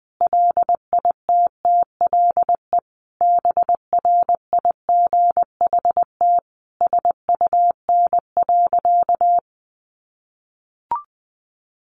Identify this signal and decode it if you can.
Morse code, heard as a keyed tone, about 20 wpm, 700 Hz. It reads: LITTLE BRIGHT SUN.
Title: Sentences from Top 1000 Words 20wpm